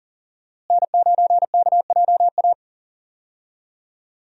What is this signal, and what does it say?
Morse code, heard as a keyed tone, about 37 wpm, 700 Hz. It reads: N9KJA
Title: USA Call Signs - 5 Characters - Repeated 2x - 40wpm